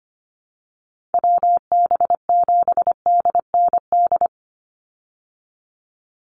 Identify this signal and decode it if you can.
Morse code, heard as a keyed tone, about 25 wpm, 700 Hz. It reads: WB7DND
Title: Call Signs Repeated 2x - 25wpm